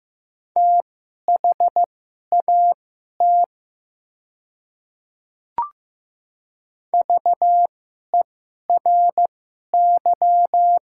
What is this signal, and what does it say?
Morse code, heard as a keyed tone, about 15 wpm, spaced as for 11 wpm, 700 Hz. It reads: THAT VERY
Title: 2x Character Spacing - Top 200 Words - 15wpm